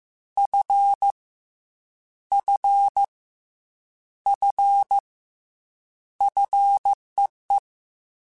Morse code, heard as a keyed tone, about 15 wpm, 800 Hz. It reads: F F F FEE
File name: Combinations / Combinations-mp3s / Combinations@13